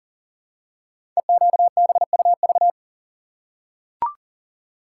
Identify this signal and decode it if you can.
Morse code, heard as a keyed tone, about 38 wpm, 700 Hz. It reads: EQBUV
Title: Code Groups - 5 Letters - 40wpm